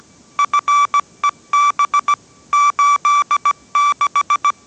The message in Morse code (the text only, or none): FEB 86